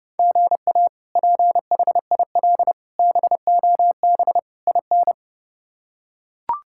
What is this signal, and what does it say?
Morse code, heard as a keyed tone, about 30 wpm, 700 Hz. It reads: GA PHIL BOB IN
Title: K1USN SST QSO Practice - 30wpm